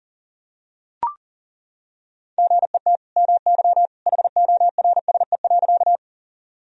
Morse code, heard as a keyed tone, about 40 wpm, 700 Hz. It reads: GET MY HORSE.
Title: Sentences from Top 1000 Words - Review Part 1 - 40wpm